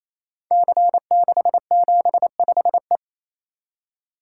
Morse code, heard as a keyed tone, about 27 wpm, 700 Hz. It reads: C675E
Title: Code Groups - 5 Letter:Numbers - 28wpm